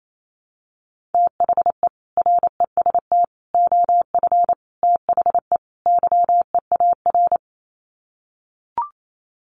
Morse code, heard as a keyed tone, about 28 wpm, 700 Hz. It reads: THE REST OF THE YEAR
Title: Sets of 5 Words Encore 28wpm